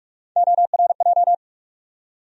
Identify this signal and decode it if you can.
Morse code, heard as a keyed tone, about 42 wpm, 700 Hz. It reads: ORJ
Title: USA Call Sign Postfixes - 3 Letters 45wpm